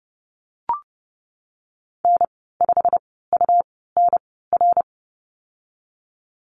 Morse code, heard as a keyed tone, about 30 wpm, spaced as for 17 wpm, 700 Hz. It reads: N5UNR
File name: Lesson 19g - Letter w - Call Signs 30wpm